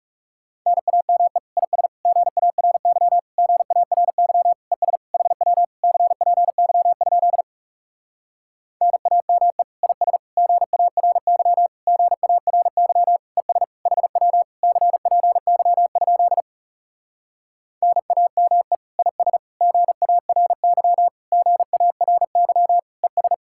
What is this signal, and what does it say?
Morse code, heard as a keyed tone, about 37 wpm, 700 Hz. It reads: NAME IS GARY GARY ES HW CPY? NAME IS GARY GARY ES HW CPY? NAME IS GARY GARY ES
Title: Speed-Racing Common QSO Phrases Encore - 30wpm